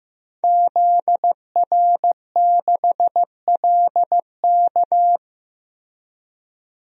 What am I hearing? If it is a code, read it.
Morse code, heard as a keyed tone, about 15 wpm, 700 Hz. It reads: ZR6LK
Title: DXCC-100 Call Signs - 5 Characters - 15wpm